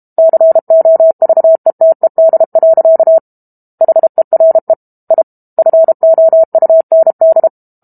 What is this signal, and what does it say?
Morse code, heard as a keyed tone, about 32 wpm, 650 Hz. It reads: COVETED. HERE I FOUND
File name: Book Two / 06-Chapter 6 - 32 WPM